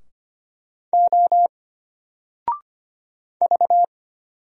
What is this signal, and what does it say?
Morse code, heard as a keyed tone, about 25 wpm, 700 Hz. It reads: O V
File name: Single-Letter-Number-25wpm